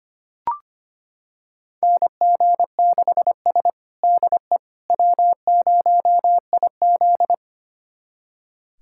Morse code, heard as a keyed tone, about 25 wpm, 700 Hz. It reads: NG6S DE W0IZ
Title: USA Callsign DE Callsign - 4 Characters - 25wpm